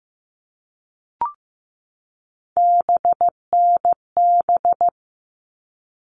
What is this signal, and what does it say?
Morse code, heard as a keyed tone, about 15 wpm, 700 Hz. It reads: BNB
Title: All Cut Numbers - 3-Digit 15wpm